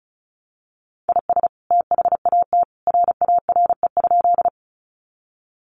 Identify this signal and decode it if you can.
Morse code, heard as a keyed tone, about 34 wpm, 700 Hz. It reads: IS THAT RARE?
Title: Sentences from Top 1100 Words 35wpm